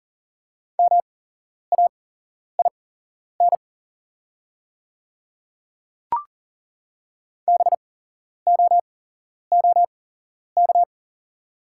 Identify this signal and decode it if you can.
Morse code, heard as a keyed tone, about 40 wpm, spaced as for 11 wpm, 690 Hz. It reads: MAIN BOOK
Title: 8x Character Spacing - Top 300 Words - 40wpm